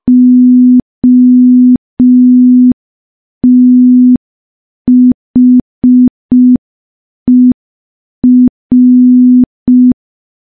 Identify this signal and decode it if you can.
Morse code, heard as a keyed tone, about 5 wpm, 250 Hz. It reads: OTHER
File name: morse / morse-training-05WPM